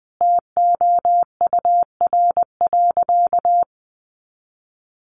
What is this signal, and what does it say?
Morse code, heard as a keyed tone, about 20 wpm, 700 Hz. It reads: TOUR.
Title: Sentences from Top 1400 Words 20wpm